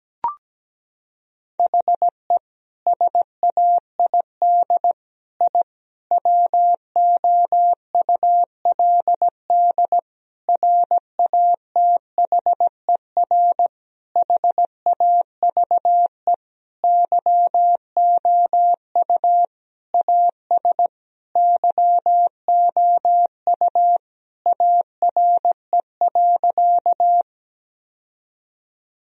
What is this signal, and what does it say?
Morse code, heard as a keyed tone, about 17 wpm, 700 Hz. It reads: HE SAID I WOULD RATHER HAVE YOU AS YOU ARE.
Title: Sentences from Top 1000 Words - Review Part 4 - 17wpm